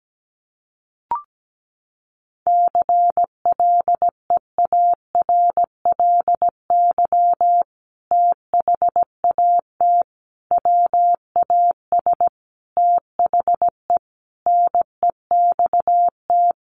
Morse code, heard as a keyed tone, about 17 wpm, 700 Hz. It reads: CLEARLY THAT WAS THE NEXT
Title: Sentences from Top 1300 Words 17wpm